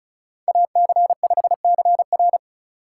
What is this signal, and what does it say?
Morse code, heard as a keyed tone, about 34 wpm, 700 Hz. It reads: AC5CR
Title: USA Call Signs - 5 Characters - 4x Character Spacing - 35wpm